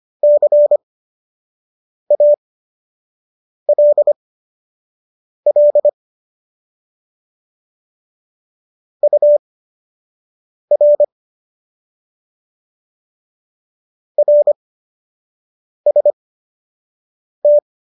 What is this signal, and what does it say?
Morse code, heard as a keyed tone, about 25 wpm, spaced as for 6 wpm, 590 Hz. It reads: CALL UR RST
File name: qso002 06